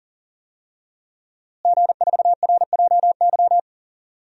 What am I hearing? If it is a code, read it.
Morse code, heard as a keyed tone, about 37 wpm, 700 Hz. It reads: G4RJY
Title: DXCC-100 Call Signs - 5 Characters - Repeated 3x - 40wpm